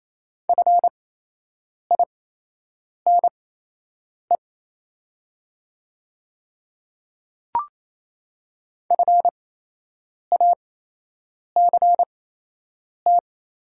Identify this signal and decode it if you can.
Morse code, heard as a keyed tone, about 27 wpm, spaced as for 8 wpm, 700 Hz. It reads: FINE FACT
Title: 8x Character Spacing - Top 500 Words - 28wpm